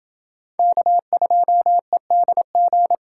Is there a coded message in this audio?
Yes: Morse code from 0.6 to 3.0 s, heard as a keyed tone, about 28 wpm, 700 Hz. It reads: K2EDG